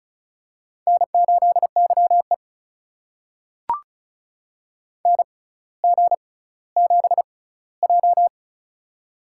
Morse code, heard as a keyed tone, about 36 wpm, spaced as for 18 wpm, 700 Hz. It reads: N8YE NG7J
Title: USA Call Signs - 4 Characters - 6x Character Spacing - 35wpm